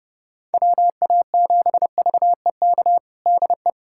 Morse code, heard as a keyed tone, about 30 wpm, 700 Hz. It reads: WA7VEK DE
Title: USA Callsign DE Callsign - 6 Characters - 30wpm